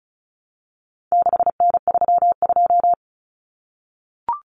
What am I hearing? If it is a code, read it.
Morse code, heard as a keyed tone, about 34 wpm, 700 Hz. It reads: 6N32
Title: Common Cut Numbers - 4-Digit 35wpm